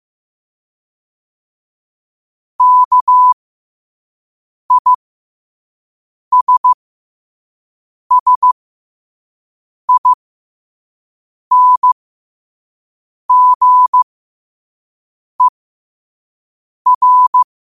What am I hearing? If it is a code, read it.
Morse code, heard as a keyed tone, about 15 wpm, spaced as for 5 wpm, 1000 Hz. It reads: KISSINGER
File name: qotdfiles / 5wpm